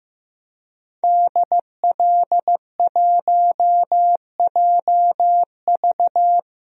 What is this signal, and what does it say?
Morse code, heard as a keyed tone, about 15 wpm, 700 Hz. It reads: DL1JV
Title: DXCC-100 Call Signs - 5 Characters - 15wpm